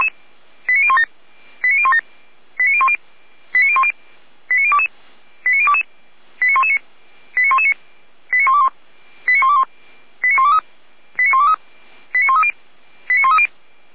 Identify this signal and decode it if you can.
Five-tone selective call (ZVEI), heard as quick runs of five tones, tones of about 70 ms. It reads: 89017 89017 89010 89010 89022 89022 89119 89119 89121 89121 89123 89123 89130 89130